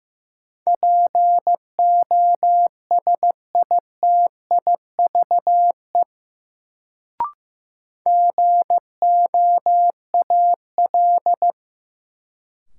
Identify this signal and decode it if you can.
Morse code, heard as a keyed tone, about 15 wpm, 700 Hz. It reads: POSITIVE GOAL
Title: Sentences from Top 1000 Words - Review Part 5 - New Word Drill - 15wpm